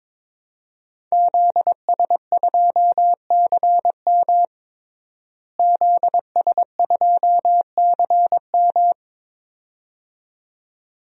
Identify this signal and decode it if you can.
Morse code, heard as a keyed tone, about 22 wpm, 700 Hz. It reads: ZS2CM ZS2CM
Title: DXCC-100 Call Signs - 5 Characters - Repeated 2x - 22wpm